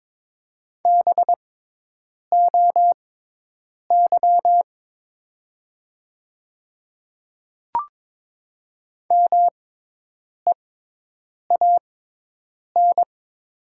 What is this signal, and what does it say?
Morse code, heard as a keyed tone, about 22 wpm, spaced as for 8 wpm, 700 Hz. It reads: BOY MEAN
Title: 6x Character Spacing - Top 200 Words - 22wpm